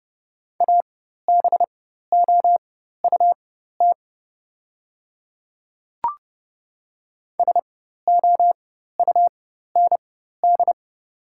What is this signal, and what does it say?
Morse code, heard as a keyed tone, about 30 wpm, spaced as for 14 wpm, 700 Hz. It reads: ABOUT SOUND
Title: 4x Character Spacing - Top 200 Words - 30wpm